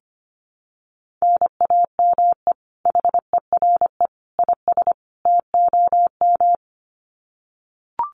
Morse code, heard as a keyed tone, about 25 wpm, 700 Hz. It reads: NAME HERE IS TOM
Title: Common QSO Phrases 25wpm